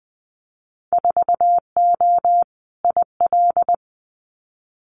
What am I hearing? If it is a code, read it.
Morse code, heard as a keyed tone, about 20 wpm, 700 Hz. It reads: 4O IL
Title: Winter Field Day Exchanges Encore 20wpm